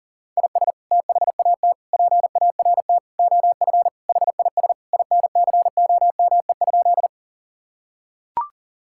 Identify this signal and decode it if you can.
Morse code, heard as a keyed tone, about 40 wpm, 700 Hz. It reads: IS THAT PART OF HIS INCOME?
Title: Sentences from Top 1000 Words - Review Part 5 - 40wpm